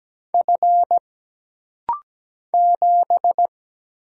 Morse code, heard as a keyed tone, about 17 wpm, 700 Hz. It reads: F 7
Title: Single Letter-Number - Rapid-Fire 17wpm